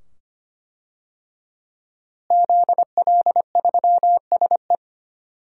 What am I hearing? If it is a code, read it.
Morse code, heard as a keyed tone, about 25 wpm, 700 Hz. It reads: ZL3SE
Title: DXCC-100 Call Signs - 5 Characters - Repeated 2x - 25wpm